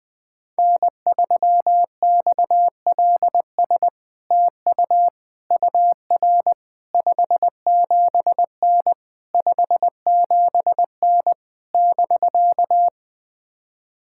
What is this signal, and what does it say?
Morse code, heard as a keyed tone, about 20 wpm, 700 Hz. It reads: N3XLS TU UR 57N 57N <BK>
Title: POTA QSO Practice - 20wpm